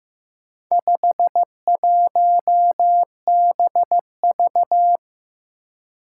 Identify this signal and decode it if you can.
Morse code, heard as a keyed tone, about 15 wpm, 700 Hz. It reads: 51BV
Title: Four Challenging Letter-Number Combinations 15wpm